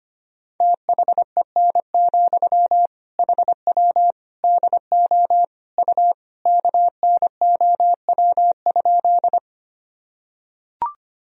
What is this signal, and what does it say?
Morse code, heard as a keyed tone, about 25 wpm, 700 Hz. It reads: THEN, HW DO U KNOW?
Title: Sentences from Top 100 Words and Abbreviations 25wpm